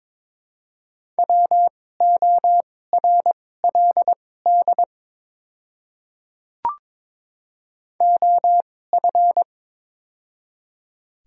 Morse code, heard as a keyed tone, about 22 wpm, spaced as for 16 wpm, 700 Hz. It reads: WORLD OF
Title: 2x Character Spacing - Top 200 Words - 22wpm